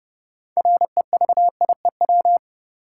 Morse code, heard as a keyed tone, about 30 wpm, 700 Hz. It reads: REVIEW
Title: Sentences from Top 900 Words - New Word Drill - 30wpm